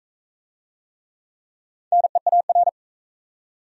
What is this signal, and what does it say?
Morse code, heard as a keyed tone, about 40 wpm, 700 Hz. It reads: NEAR